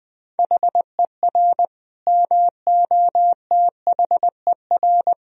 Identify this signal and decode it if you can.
Morse code, heard as a keyed tone, about 20 wpm, 700 Hz. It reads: HER MOTHER
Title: Sentences from Top 1000 Words - Review Part 4 - 20wpm